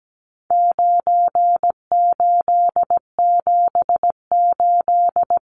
Morse code, heard as a keyed tone, about 17 wpm, 700 Hz. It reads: 9878
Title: Numbers - 4-Digit 17wpm